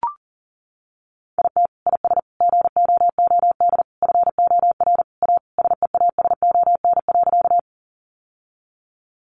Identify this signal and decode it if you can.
Morse code, heard as a keyed tone, about 40 wpm, 700 Hz. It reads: IT IS GOOD FOR A SEASON.
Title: Sentences from Top 1400 Words 40wpm